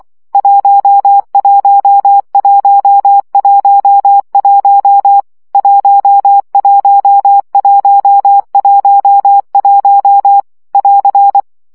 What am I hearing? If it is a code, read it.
Morse code, heard as a keyed tone, about 24 wpm, 800 Hz. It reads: 11111 11111 <AR>
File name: CW gyakorlas 1 20 baud